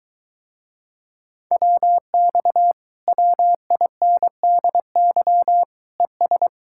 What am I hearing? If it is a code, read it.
Morse code, heard as a keyed tone, about 23 wpm, 700 Hz. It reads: WX WINDY ES